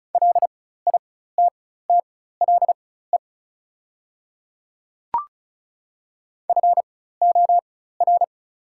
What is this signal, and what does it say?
Morse code, heard as a keyed tone, about 34 wpm, spaced as for 16 wpm, 690 Hz. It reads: LITTLE FOR
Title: 4x Character Spacing - Top 200 Words - 35wpm